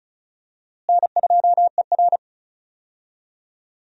Morse code, heard as a keyed tone, about 36 wpm, 700 Hz. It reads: N2ER
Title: USA Call Signs - 4 Characters - Repeated 3x - 35wpm